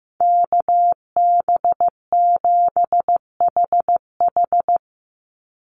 Morse code, heard as a keyed tone, about 15 wpm, 700 Hz. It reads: KB7HH
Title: NA SOTA Chaser Callsigns 15wpm